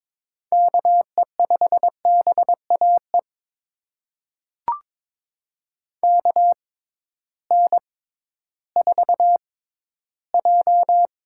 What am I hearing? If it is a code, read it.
Morse code, heard as a keyed tone, about 21 wpm, spaced as for 13 wpm, 700 Hz. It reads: KE5BAE KN4J
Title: USA Call Signs - 6 Characters - 6x Character Spacing - 22wpm